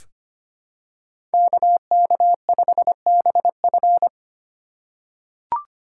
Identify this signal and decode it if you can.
Morse code, heard as a keyed tone, about 25 wpm, 700 Hz. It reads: KK5BF
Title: USA Call Signs - 5 Characters - 3x Character Spacing - 25wpm